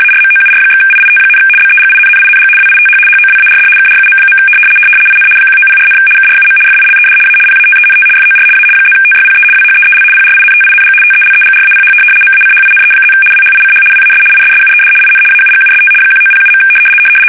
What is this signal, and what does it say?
RTTY, about 75 baud. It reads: ARMOUR/IIT, BEGINNING WITH HOWARD KRUM. IN 1904, KRUM FILED A PATENT FOR A +TYPE WHEEL PRINTING TELEGRAPH MACHINE+ WHICH WAS ISSUED IN AUGUST 1907.
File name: fsk850-75B / teletype-corp